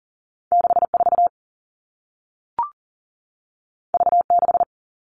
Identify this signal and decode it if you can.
Morse code, heard as a keyed tone, about 40 wpm, 700 Hz. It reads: 64 V6
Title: All Cut Numbers - 2-Digit 40wpm